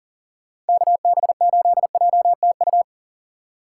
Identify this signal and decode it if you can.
Morse code, heard as a keyed tone, about 50 wpm, 700 Hz. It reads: KB8JTU
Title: Speed Racing - USA 2x3 Call Signs 40wpm